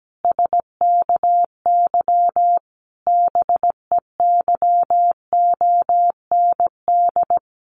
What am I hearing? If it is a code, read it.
Morse code, heard as a keyed tone, about 17 wpm, 700 Hz. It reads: SKY BEYOND
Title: A Sound Of Thunder 17wpm